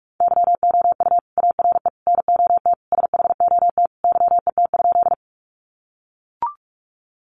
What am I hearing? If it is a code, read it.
Morse code, heard as a keyed tone, about 44 wpm, 700 Hz. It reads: YOU ARE NOT SHOT YET?
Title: Sentences from Top 1100 Words 45wpm